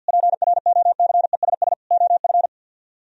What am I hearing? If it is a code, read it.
Morse code, heard as a keyed tone, about 64 wpm, 700 Hz. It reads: PROCESS OF